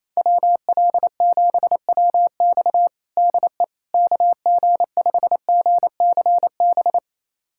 Morse code, heard as a keyed tone, about 28 wpm, 700 Hz. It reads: WL7WX DE KG5GCB
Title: USA Callsign DE Callsign 28wpm